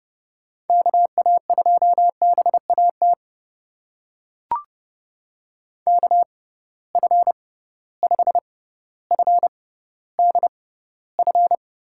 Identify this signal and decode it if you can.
Morse code, heard as a keyed tone, about 30 wpm, spaced as for 15 wpm, 700 Hz. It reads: KA2BAT KF5FDF
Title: USA Call Signs - 6 Characters - 6x Character Spacing - 30wpm